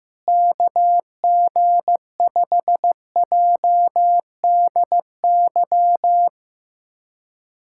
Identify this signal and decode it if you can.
Morse code, heard as a keyed tone, about 15 wpm, 700 Hz. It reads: KG5JDY
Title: USA Callsign DE Callsign 15wpm